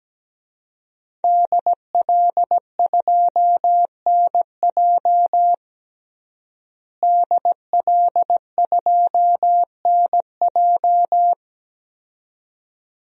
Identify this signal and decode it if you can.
Morse code, heard as a keyed tone, about 17 wpm, 700 Hz. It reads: DL2NJ DL2NJ
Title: DXCC-100 Call Signs - 5 Characters - Repeated 2x - 17wpm